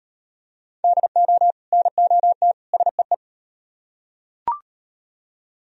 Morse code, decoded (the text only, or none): DO NOT SEE